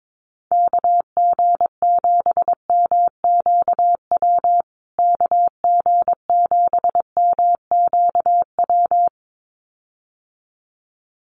Morse code, heard as a keyed tone, about 22 wpm, 700 Hz. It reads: KG7MQW KG7MQW
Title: USA Call Signs Repeated 2x 22wpm